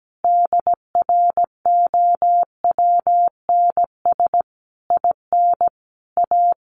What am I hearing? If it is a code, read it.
Morse code, heard as a keyed tone, about 17 wpm, 700 Hz. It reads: DROWNS IN A
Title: A Sound Of Thunder 17wpm